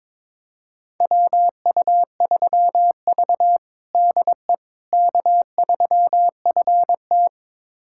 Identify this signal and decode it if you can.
Morse code, heard as a keyed tone, about 22 wpm, 700 Hz. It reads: WU3V DE K3FT
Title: USA Callsign DE Callsign - 4 Characters - 22wpm